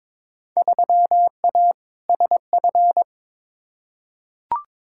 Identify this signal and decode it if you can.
Morse code, heard as a keyed tone, about 22 wpm, 700 Hz. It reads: 3A SF
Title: Field Day Exchanges Encore - 22wpm